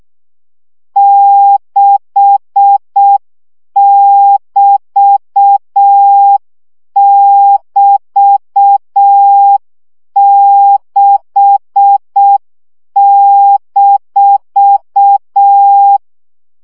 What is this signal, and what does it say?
Morse code, heard as a keyed tone, about 6 wpm, 800 Hz. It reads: 6<BT><BT>6-
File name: CW gyakorlas b 6 egyenlo minusz 5 baud